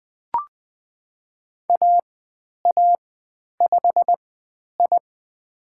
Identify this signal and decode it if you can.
Morse code, heard as a keyed tone, about 20 wpm, spaced as for 10 wpm, 700 Hz. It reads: AA5I
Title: 095 Lesson 16g - Letter c - Call Signs 20wpm